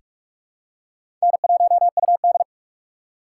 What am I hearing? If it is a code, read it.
Morse code, heard as a keyed tone, about 46 wpm, 700 Hz. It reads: N1UD
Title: USA Call Signs - 2x Character Spacing - 45wpm